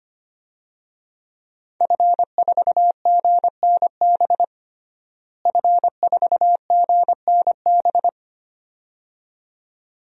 Morse code, heard as a keyed tone, about 25 wpm, 700 Hz. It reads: F4GNB F4GNB
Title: DXCC-100 Call Signs - 5 Characters - Repeated 2x - 25wpm